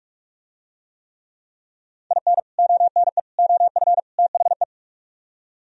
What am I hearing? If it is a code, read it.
Morse code, heard as a keyed tone, about 45 wpm, 700 Hz. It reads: IN ONE OF THE